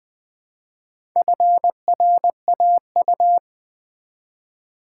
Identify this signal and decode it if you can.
Morse code, heard as a keyed tone, about 20 wpm, 700 Hz. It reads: FRAU
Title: Build A Word - 1 Letter At A Time - 3 to 5-Letter Words - 20wpm